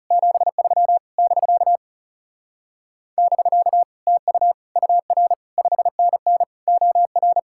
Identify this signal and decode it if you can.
Morse code, heard as a keyed tone, about 37 wpm, 700 Hz. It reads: 73 <BK> <BK> TU UR 5NN OR
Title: Speed-Racing Common QSO Phrases - 30wpm